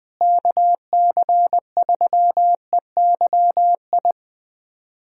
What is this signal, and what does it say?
Morse code, heard as a keyed tone, about 20 wpm, 700 Hz. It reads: KC3EYI
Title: USA Callsign DE Callsign - 6 Characters - 20wpm